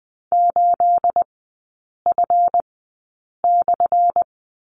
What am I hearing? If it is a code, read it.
Morse code, heard as a keyed tone, about 20 wpm, spaced as for 8 wpm, 700 Hz. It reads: 8F/
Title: 212 Lesson 34e - Number 8 - Three Characters 20wpm